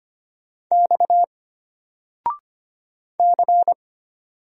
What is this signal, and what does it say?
Morse code, heard as a keyed tone, about 25 wpm, 700 Hz. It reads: X C
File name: Single-Letter-Number-25wpm